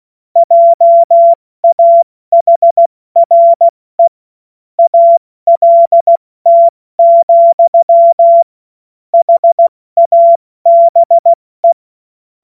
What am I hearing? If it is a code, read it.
Morse code, heard as a keyed tone, about 16 wpm, 680 Hz. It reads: JAHRE ALT, HABE
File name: Morsekurs / Witze / Witz 03. August 2021, 70 F80